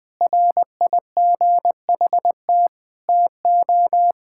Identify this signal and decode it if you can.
Morse code, heard as a keyed tone, about 21 wpm, 700 Hz. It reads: RIGHT TO
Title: Sentences with 3-Word Phrases - 20wpm